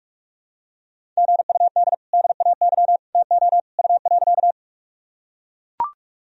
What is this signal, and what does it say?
Morse code, heard as a keyed tone, about 43 wpm, 700 Hz. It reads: GUD DAY TO U.
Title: Sentences from Top 200 Words and Abbreviations 45wpm